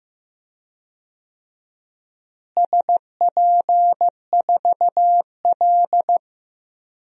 Morse code, heard as a keyed tone, about 15 wpm, 700 Hz. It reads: SP4L
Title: DXCC-100 Call Signs - 4 Characters - Repeated 3x - 15wpm